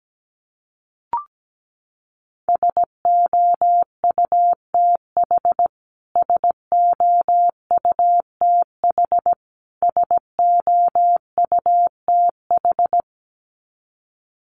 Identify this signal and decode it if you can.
Morse code, heard as a keyed tone, about 17 wpm, 700 Hz. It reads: SOUTH SOUTH SOUTH
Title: Top 500 Words Repeated 17wpm